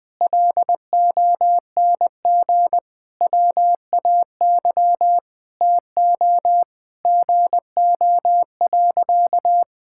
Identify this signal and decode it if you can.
Morse code, heard as a keyed tone, about 20 wpm, 700 Hz. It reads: LONG WAY TO GO.
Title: Sentences with 6-Word Phrases - 20wpm